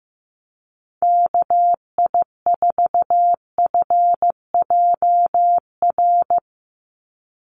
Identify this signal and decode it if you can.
Morse code, heard as a keyed tone, about 15 wpm, 700 Hz. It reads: KI4FJR
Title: USA Call Signs 15wpm